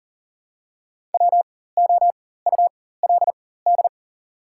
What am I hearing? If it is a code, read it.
Morse code, heard as a keyed tone, about 42 wpm, spaced as for 19 wpm, 700 Hz. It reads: WOULD
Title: 4x Character Spacing - Top 200 Words - 40wpm